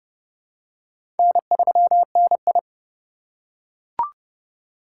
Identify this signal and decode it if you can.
Morse code, heard as a keyed tone, about 29 wpm, 700 Hz. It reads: N3NI
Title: USA Call Signs - 4 Characters - 3x Character Spacing - 30wpm